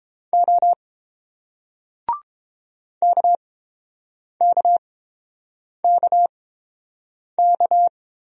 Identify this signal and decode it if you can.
Morse code, heard as a keyed tone, about 30 wpm, spaced as for 7 wpm, 700 Hz. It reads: O KKKK